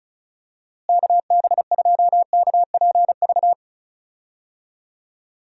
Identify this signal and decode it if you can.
Morse code, heard as a keyed tone, about 34 wpm, 700 Hz. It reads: KB2KPV
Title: USA Call Signs - 6 Characters - Repeated 3x - 35wpm